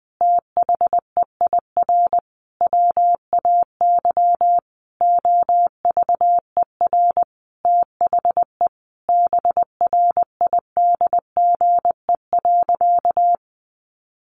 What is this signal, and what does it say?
Morse code, heard as a keyed tone, about 20 wpm, 700 Hz. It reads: THEIR WAY OVER THE BRIDGE.
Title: Sentences from Top 1500 Words 20wpm